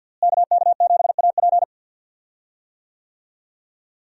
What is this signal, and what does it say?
Morse code, heard as a keyed tone, about 46 wpm, 700 Hz. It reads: KK7AP
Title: USA Call Signs - 5 Characters - Repeated 3x - 50wpm